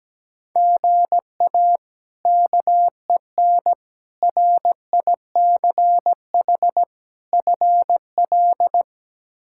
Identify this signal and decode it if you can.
Morse code, heard as a keyed tone, about 17 wpm, 700 Hz. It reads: GA KEN RICH FL
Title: K1USN SST QSO Practice - 17wpm